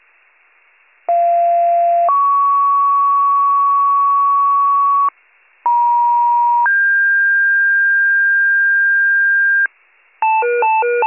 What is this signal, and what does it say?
Two-tone page, heard as a one-second tone, then a three-second tone, about 690 then 1120 Hz, about 950 then 1600 Hz, with noise. It is heard from 1.1 to 9.7 s.